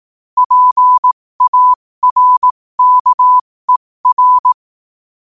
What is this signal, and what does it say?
Morse code, heard as a keyed tone, about 18 wpm, spaced as for 15 wpm, 1000 Hz. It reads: PARKER
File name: qotdfiles / 15wpm